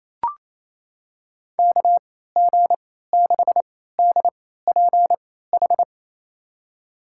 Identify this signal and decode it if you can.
Morse code, heard as a keyed tone, about 28 wpm, spaced as for 16 wpm, 700 Hz. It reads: KG6DPH
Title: USA Call Signs - 3x Character Spacing - 28wpm